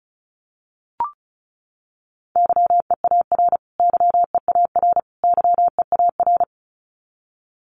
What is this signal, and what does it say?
Morse code, heard as a keyed tone, about 34 wpm, 700 Hz. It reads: YEAR YEAR YEAR
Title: Top 500 Words Repeated 35wpm